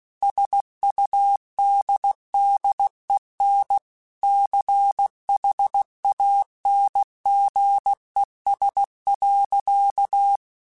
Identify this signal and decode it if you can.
Morse code, heard as a keyed tone, about 16 wpm, 800 Hz. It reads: SUDDEN CHANGES.
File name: Technical Subjects / Technical Subjects-mp3s / Reactance @15